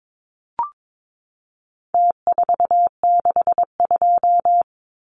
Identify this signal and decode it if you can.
Morse code, heard as a keyed tone, about 22 wpm, 700 Hz. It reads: T462
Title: Numbers - Contest Style - 4-Digit 22wpm